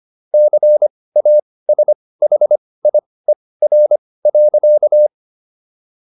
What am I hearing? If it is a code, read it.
Morse code, heard as a keyed tone, about 25 wpm, spaced as for 18 wpm, 600 Hz. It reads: CASHIER.